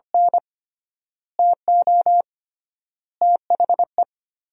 Morse code, heard as a keyed tone, about 25 wpm, 700 Hz. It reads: N TO THE
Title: 3x Word-Spacing - Sentences from Top 300 Words - 25wpm